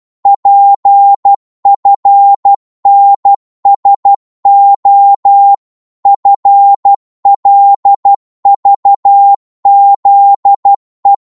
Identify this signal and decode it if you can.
Morse code, heard as a keyed tone, about 12 wpm, 800 Hz. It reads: PFNSO FLVZE
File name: Morse / ltr-12wpm